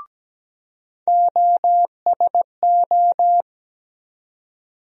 Morse code, heard as a keyed tone, about 17 wpm, 700 Hz. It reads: OSO
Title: USA Call Sign Postfixes - 3 Letters 17wpm